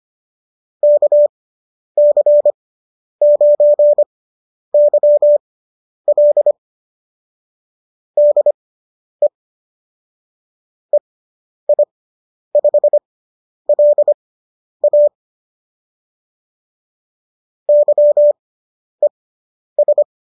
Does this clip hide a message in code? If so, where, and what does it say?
Yes: Morse code from 0.8 to 20.0 s, heard as a keyed tone, about 25 wpm, spaced as for 10 wpm, 590 Hz. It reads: KC9YL DE EI5LA YES